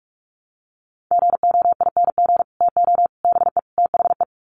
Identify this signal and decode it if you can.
Morse code, heard as a keyed tone, about 44 wpm, 700 Hz. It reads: GOING TO BE THE
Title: Sets of 4 Words 45wpm